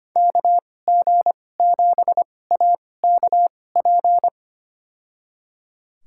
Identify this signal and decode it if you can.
Morse code, heard as a keyed tone, about 25 wpm, spaced as for 18 wpm, 700 Hz. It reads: KG7AKP
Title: USA Call Signs - 6 Characters - 2x Character Spacing - 25wpm